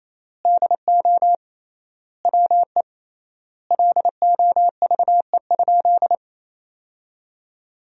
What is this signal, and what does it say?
Morse code, heard as a keyed tone, about 28 wpm, 700 Hz. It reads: DO WE LOVE?
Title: 3x Word-Spacing - Sentences from Top 400 Words - 28wpm